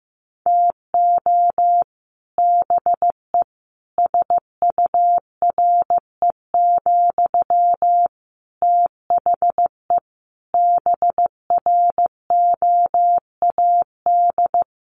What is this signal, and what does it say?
Morse code, heard as a keyed tone, about 15 wpm, 700 Hz. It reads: TO BE SURE, THE BROAD
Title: Sentences from Top 1000 Words 15wpm